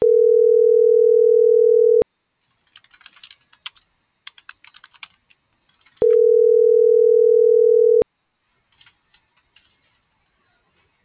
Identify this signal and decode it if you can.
Ringback tone (North America), 440 + 480 Hz.